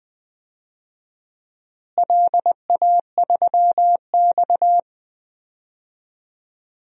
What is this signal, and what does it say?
Morse code, heard as a keyed tone, about 21 wpm, 700 Hz. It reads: LA3X
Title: DXCC-100 Call Signs - 4 Characters - Repeated 2x - 20wpm